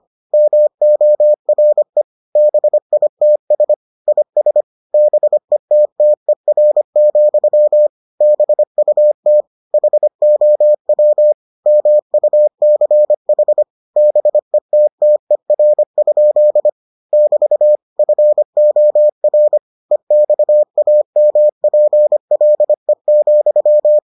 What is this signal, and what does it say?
Morse code, heard as a keyed tone, about 25 wpm, 600 Hz. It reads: MORE BITS IS BETTER, BUT HOW MUCH BETTER? <BT> FOR EXAMPLE,